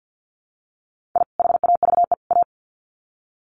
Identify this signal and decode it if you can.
Morse code, heard as a keyed tone, about 50 wpm, 700 Hz. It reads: I HAVE A
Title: Sets of 3 Words 50wpm